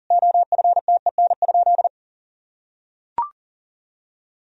Morse code, heard as a keyed tone, about 40 wpm, 700 Hz. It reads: OFTEN?
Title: Sentences from Top 1000 Words - Review Part 1 - 40wpm